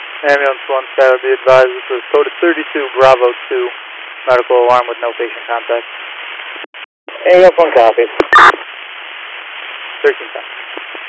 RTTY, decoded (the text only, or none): none